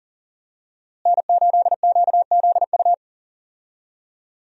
Morse code, heard as a keyed tone, about 40 wpm, 700 Hz. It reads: N8QZU
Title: USA Call Signs - 5 Characters - Repeated 3x - 40wpm